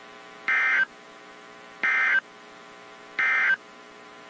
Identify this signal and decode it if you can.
Emergency Alert System (SAME) tones, heard as harsh buzzing bursts. It is heard from 0.4 to 3.6 s.
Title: WZ2523 - NOAA Weather Radio